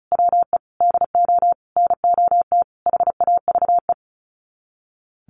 Morse code, heard as a keyed tone, about 34 wpm, 700 Hz. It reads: WE DO NOT HAVE
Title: Sets of 5 Words Encore2 35wpm